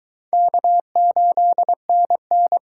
Morse code, heard as a keyed tone, about 22 wpm, 700 Hz. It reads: K8NN